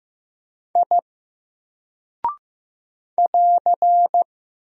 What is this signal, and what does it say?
Morse code, heard as a keyed tone, about 15 wpm, 700 Hz. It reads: I <AR>
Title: Single Character (Letter:Number:Punctuation:Prosign) 15wpm